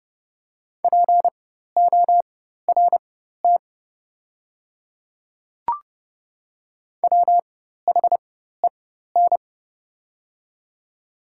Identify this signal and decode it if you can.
Morse code, heard as a keyed tone, about 30 wpm, spaced as for 14 wpm, 700 Hz. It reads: PORT WHEN
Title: 4x Character Spacing - Top 200 Words - 30wpm